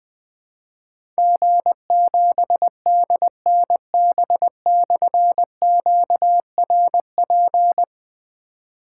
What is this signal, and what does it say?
Morse code, heard as a keyed tone, about 20 wpm, 700 Hz. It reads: G7DNB/QRP
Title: Difficult Call Signs 20wpm